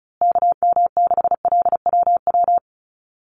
Morse code, heard as a keyed tone, about 34 wpm, 700 Hz. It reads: KM6LWW
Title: USA Call Signs Repeated 2x 35wpm